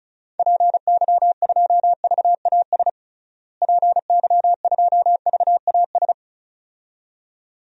Morse code, heard as a keyed tone, about 34 wpm, 700 Hz. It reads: PY2VAS PY2VAS
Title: DXCC-100 Call Signs - 6 Characters - Repeated 2x - 35wpm